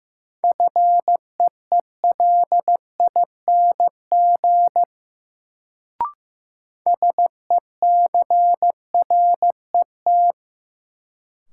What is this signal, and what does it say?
Morse code, heard as a keyed tone, about 15 wpm, 700 Hz. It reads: FEELING SECRET
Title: Sentences from Top 1200 Words - New Word Drill Alt - 15wpm